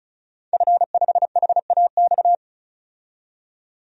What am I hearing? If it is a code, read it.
Morse code, heard as a keyed tone, about 34 wpm, 700 Hz. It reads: F5HAX
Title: DXCC-100 Call Signs - 5 Characters - 35wpm